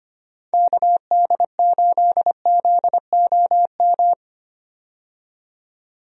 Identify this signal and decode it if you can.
Morse code, heard as a keyed tone, about 25 wpm, 700 Hz. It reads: KD8ZOM
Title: USA Call Signs - 6 Characters - 25wpm